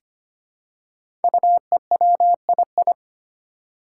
Morse code, heard as a keyed tone, about 25 wpm, 700 Hz. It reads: UEWII
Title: Five Challenging Letter-Number Combinations 25wpm